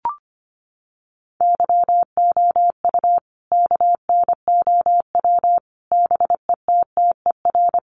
Morse code, heard as a keyed tone, about 25 wpm, 700 Hz. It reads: YOU KNOW BETTER
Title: Sentences from Top 400 Words 25wpm